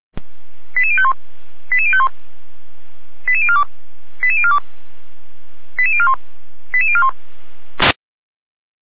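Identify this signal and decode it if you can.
Five-tone selective call (ZVEI), heard as quick runs of five tones, tones of about 80 ms. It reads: 88051 88051 88042 88042 88041 88041
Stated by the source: Five-tone ZVEI selective call 88129, received on an SDR, from the Feuerwehr Orlamünde fire brigade alerting archive.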